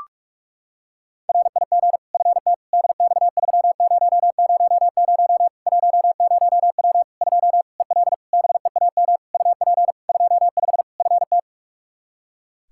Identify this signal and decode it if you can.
Morse code, heard as a keyed tone, about 46 wpm, 700 Hz. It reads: RIG FT DX3000 10W 2 EL BEAM UP 25 FT
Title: Practice QSO - Exchange 2 - Info - 45wpm